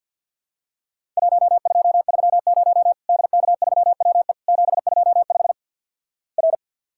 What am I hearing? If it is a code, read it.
Morse code, heard as a keyed tone, about 50 wpm, 700 Hz. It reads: 1230 DK3WE 725 R
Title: Worked All Europe DX Contest - QTC Exchange with Repeats - 50wpm